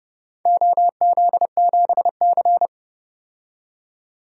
Morse code, heard as a keyed tone, about 30 wpm, 700 Hz. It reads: OZ7C
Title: DXCC-100 Call Signs - 4 Characters - 30wpm